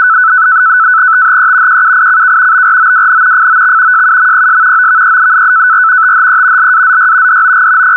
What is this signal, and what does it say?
NAVTEX (SITOR-B), about 100 baud. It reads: THE QUICK BROWN FOX JUMPS OVER THE LAZY DOG
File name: AMTOR-FEC-test-message